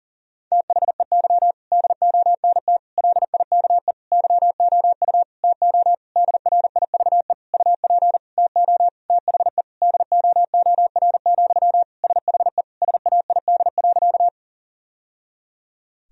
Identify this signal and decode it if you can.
Morse code, heard as a keyed tone, about 40 wpm, 700 Hz. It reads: THEY DONT LIKE YOU TO DRIVE UP TO THE DOOR, SHE SAID.
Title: Sentences from Top 1000 Words - Review Part 2 - 40wpm